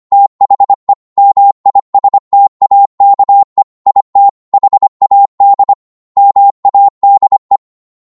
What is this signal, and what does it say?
Morse code, heard as a keyed tone, about 25 wpm, 800 Hz. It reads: THE MISTAKE IT HAD MADE
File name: 25wpm